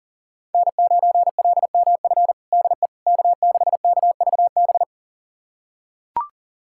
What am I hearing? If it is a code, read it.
Morse code, heard as a keyed tone, about 40 wpm, 700 Hz. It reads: N9LMF DE K6KVB
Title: USA Callsign DE Callsign - 5 Characters - 40wpm